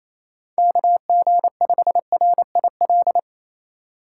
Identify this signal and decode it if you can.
Morse code, heard as a keyed tone, about 28 wpm, 700 Hz. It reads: KG5RIL
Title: USA Call Signs - 6 Characters - 2x Character Spacing - 28wpm